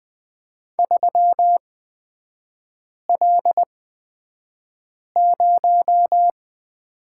Morse code, heard as a keyed tone, about 20 wpm, spaced as for 5 wpm, 700 Hz. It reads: 3L0
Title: 250 Lesson 40e - BK - Three Characters 20wpm